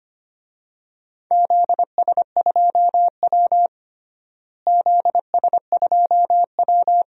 Morse code, heard as a keyed tone, about 25 wpm, 700 Hz. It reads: ZS2W ZS2W
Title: DXCC-100 Call Signs - 4 Characters - Repeated 2x - 25wpm